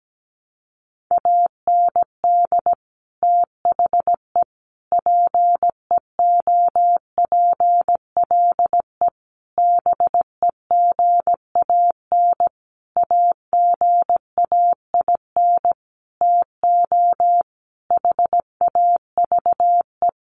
Morse code, heard as a keyed tone, about 17 wpm, 700 Hz. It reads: AND THE PEOPLE BEGAN AGAIN TO HAVE
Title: Sentences from Top 1500 Words 17wpm